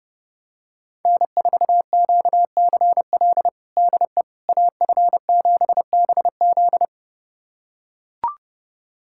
Morse code, heard as a keyed tone, about 30 wpm, 700 Hz. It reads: N4QCL DE AF7BZ
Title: USA Callsign DE Callsign - 5 Characters - 30wpm